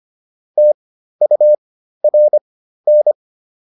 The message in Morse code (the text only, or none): TURN